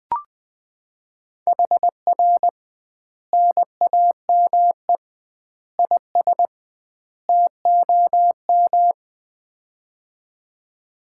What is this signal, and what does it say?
Morse code, heard as a keyed tone, about 21 wpm, 700 Hz. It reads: HR NAME IS TOM
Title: Common QSO Phrases - 2x Word-Spacing - 20wpm